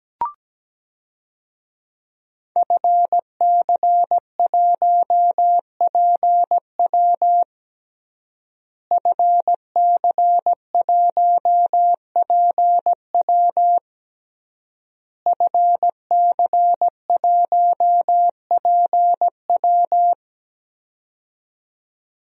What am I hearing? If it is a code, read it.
Morse code, heard as a keyed tone, about 17 wpm, 700 Hz. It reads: FC1PW FC1PW FC1PW
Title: DXCC-100 Call Signs - 5 Characters - Repeated 3x - 17wpm